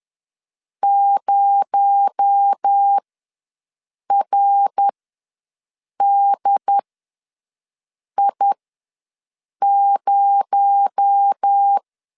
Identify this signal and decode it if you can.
Morse code, heard as a keyed tone, about 10 wpm, spaced as for 5 wpm, 800 Hz. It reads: 0RDI0
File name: LEZIONI / 6 PKX0 / 6 PKX0 ws 10wpm 4220